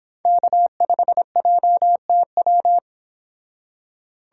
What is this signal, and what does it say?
Morse code, heard as a keyed tone, about 25 wpm, 700 Hz. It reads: K5JTW